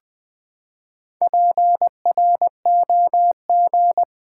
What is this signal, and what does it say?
Morse code, heard as a keyed tone, about 20 wpm, 700 Hz. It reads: PROG
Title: Build A Word - 4 Letters At A Time - 11-Letter Words - 20wpm